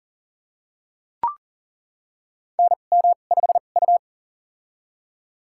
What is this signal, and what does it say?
Morse code, heard as a keyed tone, about 38 wpm, spaced as for 28 wpm, 690 Hz. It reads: NM5U
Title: USA Call Signs - 4 Characters - 2x Character Spacing - 40wpm